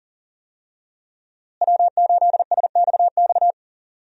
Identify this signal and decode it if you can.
Morse code, heard as a keyed tone, about 40 wpm, 700 Hz. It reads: W8SXX
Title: USA Call Signs - 5 Characters - Repeated 3x - 40wpm